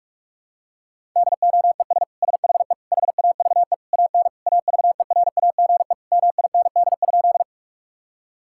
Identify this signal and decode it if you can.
Morse code, heard as a keyed tone, about 46 wpm, 700 Hz. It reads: DOES SHE HAVE AN AVERAGE MIND?
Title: Sentences from Top 1000 Words - Review Part 6 - 45wpm